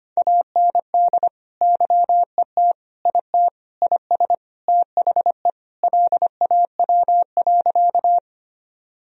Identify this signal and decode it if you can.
Morse code, heard as a keyed tone, about 25 wpm, 700 Hz. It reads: AND YET IT IS THE LAW.
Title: Sentences from Top 1000 Words - Review Part 3 - 25wpm